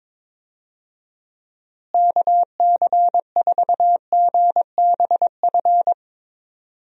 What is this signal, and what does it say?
Morse code, heard as a keyed tone, about 22 wpm, 700 Hz. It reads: KC4GBF
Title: USA Call Signs - 6 Characters - Repeated 3x - 22wpm